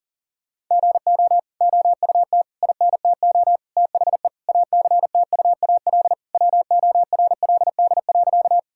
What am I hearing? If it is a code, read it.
Morse code, heard as a keyed tone, about 40 wpm, 700 Hz. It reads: GO OUT INTO THE ACTUAL WORLD.
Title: Sentences from Top 1000 Words - Review Part 5 - 40wpm